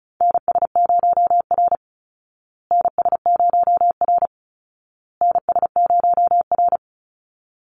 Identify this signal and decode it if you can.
Morse code, heard as a keyed tone, about 34 wpm, 700 Hz. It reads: NS0R NS0R NS0R
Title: Call Signs Repeated 2x - 35wpm